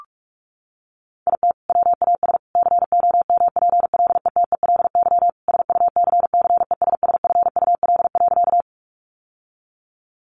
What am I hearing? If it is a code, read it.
Morse code, heard as a keyed tone, about 46 wpm, 700 Hz. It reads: IT WAS COMPLETELY SUCCESSFUL.
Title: Sentences from Top 1100 Words 45wpm